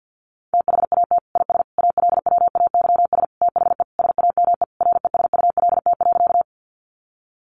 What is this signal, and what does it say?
Morse code, heard as a keyed tone, about 50 wpm, 700 Hz. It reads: THAT IS ALWAYS THE SAME RESULT.
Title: Sentences from Top 700 Words 50wpm